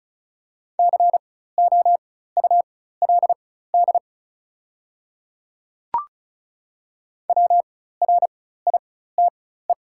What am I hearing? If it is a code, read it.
Morse code, heard as a keyed tone, about 36 wpm, spaced as for 16 wpm, 690 Hz. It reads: COULD WRITE
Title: 4x Character Spacing - Top 100 Words - 35wpm